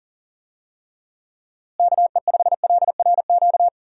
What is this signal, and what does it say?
Morse code, heard as a keyed tone, about 40 wpm, 700 Hz. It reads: KE5LRQ
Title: USA Call Signs - 6 Characters - Repeated 2x - 40wpm